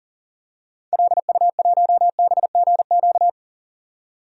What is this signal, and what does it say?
Morse code, heard as a keyed tone, about 40 wpm, 700 Hz. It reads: LU1BGQ
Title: DXCC-100 Call Signs - 6 Characters - 40wpm